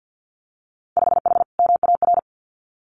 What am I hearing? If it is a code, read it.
Morse code, heard as a keyed tone, about 50 wpm, 700 Hz. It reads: 5H MAR
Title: Winter Field Day Exchanges Encore 50wpm